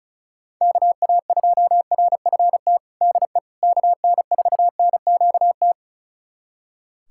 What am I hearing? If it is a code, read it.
Morse code, heard as a keyed tone, about 36 wpm, 700 Hz. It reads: KA2RFT DE KN4NQT
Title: USA Callsign DE Callsign 35wpm